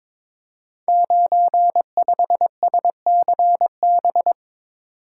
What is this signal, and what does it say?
Morse code, heard as a keyed tone, about 23 wpm, 700 Hz. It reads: 95SCB
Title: Code Groups - 5 Letter:Numbers - 22wpm